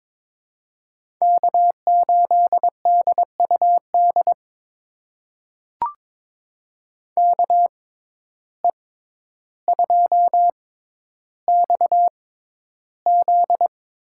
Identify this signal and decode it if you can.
Morse code, heard as a keyed tone, about 22 wpm, spaced as for 11 wpm, 700 Hz. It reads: K8DUD KE2XZ
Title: USA Call Signs - 5 Characters - 6x Character Spacing - 22wpm